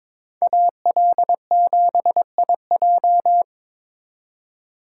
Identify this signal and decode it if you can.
Morse code, heard as a keyed tone, about 25 wpm, 700 Hz. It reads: AL7IJ
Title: Speed Racing - USA Call Signs 22wpm